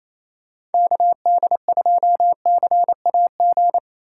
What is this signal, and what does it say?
Morse code, heard as a keyed tone, about 28 wpm, 700 Hz. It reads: KD2CAG
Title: USA Call Signs - 6 Characters - 2x Character Spacing - 28wpm